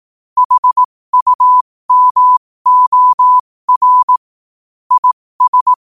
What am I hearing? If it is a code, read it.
Morse code, heard as a keyed tone, about 18 wpm, spaced as for 15 wpm, 1000 Hz. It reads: HUMOR IS
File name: qotdfiles / 15wpm